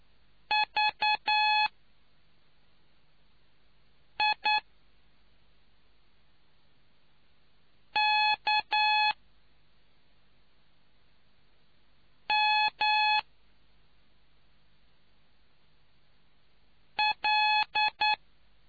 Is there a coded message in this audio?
Yes: Morse code from 0.5 to 18.1 s, heard as a keyed tone, about 9 wpm, spaced as for 2 wpm, 850 Hz. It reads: VIKML